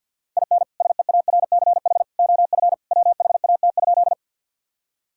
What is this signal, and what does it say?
Morse code, heard as a keyed tone, about 50 wpm, 700 Hz. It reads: IN SEARCH OF WHAT?
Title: Sentences from Top 1000 Words - Review Part 5 - 50wpm